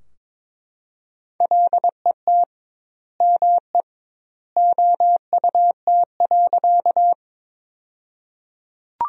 Morse code, heard as a keyed tone, about 22 wpm, 700 Hz. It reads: LET ME OUT.
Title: 2x Word-Spacing - Sentences from Top 300 Words - 22wpm